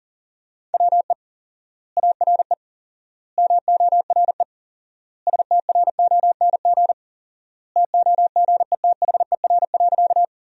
Morse code, heard as a keyed tone, about 40 wpm, 700 Hz. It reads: WE ARE MORE STRONG TOGETHER.
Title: 4x Word-Spacing - Sentences from Top 500 Words - 40wpm